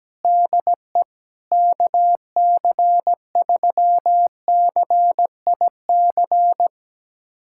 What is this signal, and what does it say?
Morse code, heard as a keyed tone, about 17 wpm, 700 Hz. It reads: DE KC3CIC
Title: USA Callsign DE Callsign - 6 Characters - 17wpm